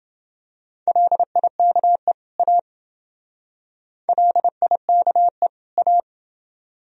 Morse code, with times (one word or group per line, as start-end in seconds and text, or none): 0.9-2.1 LIKE
2.4-2.6 A
4.1-5.5 LIKE
5.8-6.0 A